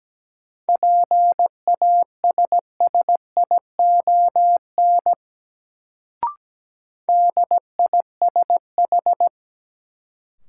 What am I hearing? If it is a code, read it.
Morse code, heard as a keyed tone, about 17 wpm, 700 Hz. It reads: PASSION DISH
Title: Sentences from Top 1300 Words - New Word Drill - 17wpm